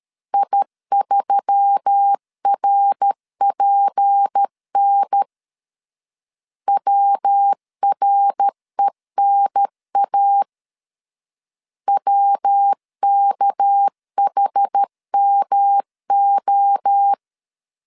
Morse code, heard as a keyed tone, about 13 wpm, 800 Hz. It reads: I3RPN WRENA WKHMO
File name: LEZIONI / 6 PKX0 / 6 PK 12wpm 2006